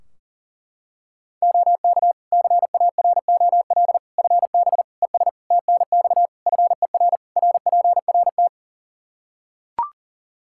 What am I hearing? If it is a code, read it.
Morse code, heard as a keyed tone, about 40 wpm, 700 Hz. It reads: OK CAROL FB ES TNX FER RPRT
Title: K1USN SST QSO Practice 40wpm